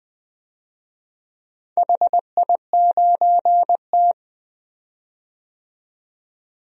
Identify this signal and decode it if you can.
Morse code, heard as a keyed tone, about 20 wpm, 700 Hz. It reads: HI9T
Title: DXCC-100 Call Signs - 4 Characters - 20wpm